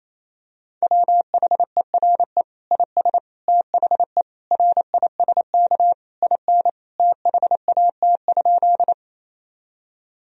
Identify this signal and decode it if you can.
Morse code, heard as a keyed tone, about 28 wpm, 700 Hz. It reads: WHERE IS THE RISK IN THAT?
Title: Sentences from Top 1000 Words - Review Part 4 - 28wpm